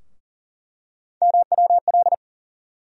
Morse code, heard as a keyed tone, about 37 wpm, 700 Hz. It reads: MWL
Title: USA Call Sign Postfixes - 3 Letters 40wpm